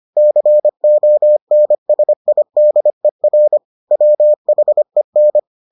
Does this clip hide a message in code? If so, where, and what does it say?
Yes: Morse code from 0.2 to 5.4 s, heard as a keyed tone, about 25 wpm, 600 Hz. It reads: CONSIDER WHEN